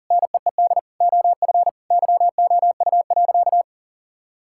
Morse code, heard as a keyed tone, about 40 wpm, 700 Hz. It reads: NEED OF YOU.
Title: Sentences from Top 1000 Words - Review Part 5 - 40wpm